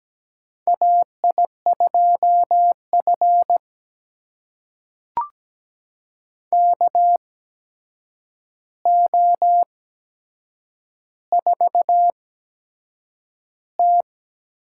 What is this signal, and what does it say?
Morse code, heard as a keyed tone, about 17 wpm, spaced as for 7 wpm, 700 Hz. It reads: AI2F KO4T
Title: USA Call Signs - 8x Character Spacing - 17wpm